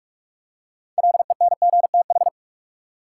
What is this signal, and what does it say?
Morse code, heard as a keyed tone, about 46 wpm, 700 Hz. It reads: LENGTH
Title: Six Letter Words - 45wpm